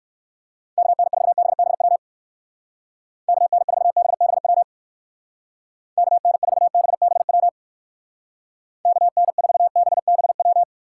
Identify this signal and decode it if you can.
Morse code, heard as a keyed tone, about 58 wpm, 700 Hz. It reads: KN4BBW KN4BBW KN4BBW KN4BBW